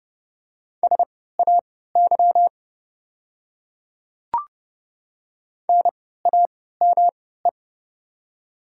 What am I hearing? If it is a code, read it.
Morse code, heard as a keyed tone, about 30 wpm, spaced as for 17 wpm, 700 Hz. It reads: SAY NAME
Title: 3x Character Spacing - Top 300 Words - 30wpm